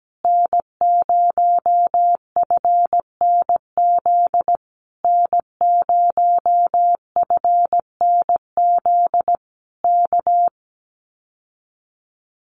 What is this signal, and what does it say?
Morse code, heard as a keyed tone, about 17 wpm, 700 Hz. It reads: N0FNZ N0FNZ K
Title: Common QSO Phrases Encore 17wpm